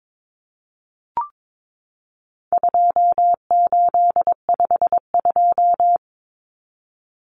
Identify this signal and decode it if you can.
Morse code, heard as a keyed tone, about 22 wpm, 700 Hz. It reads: 2852
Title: Common Cut Numbers - 4-Digit 22wpm